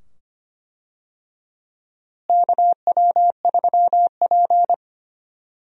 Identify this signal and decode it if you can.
Morse code, heard as a keyed tone, about 25 wpm, 700 Hz. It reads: KW3P
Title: USA Call Signs - 4 Characters - Repeated 2x - 25wpm